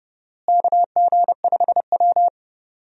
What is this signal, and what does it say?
Morse code, heard as a keyed tone, about 30 wpm, 700 Hz. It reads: KG5W
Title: USA Call Signs - 4 Characters - Repeated 2x - 30wpm